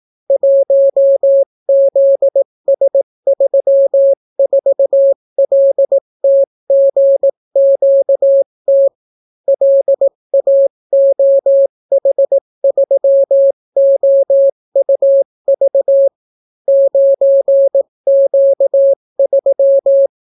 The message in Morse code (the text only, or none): 1ZS34LTGQT LAOH3OUV 9Q3